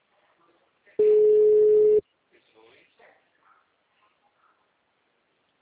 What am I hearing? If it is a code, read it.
Ringback tone (Europe, 425 Hz), with speech.